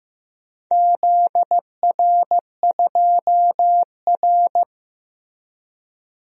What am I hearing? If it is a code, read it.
Morse code, heard as a keyed tone, about 15 wpm, 700 Hz. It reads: ZR2R
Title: DXCC-100 Call Signs - 4 Characters - Repeated 3x - 15wpm